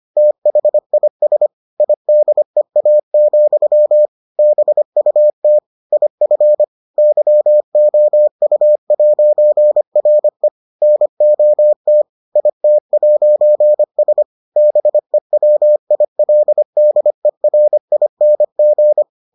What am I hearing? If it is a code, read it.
Morse code, heard as a keyed tone, about 25 wpm, 600 Hz. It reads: THIS IDEA, BUT IF YOU'RE NOT IT'S BEWILDERING